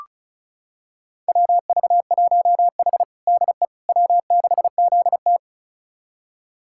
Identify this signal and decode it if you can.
Morse code, heard as a keyed tone, about 36 wpm, 700 Hz. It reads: WV1H DE W6ZT
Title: USA Callsign DE Callsign - 4 Characters - 35wpm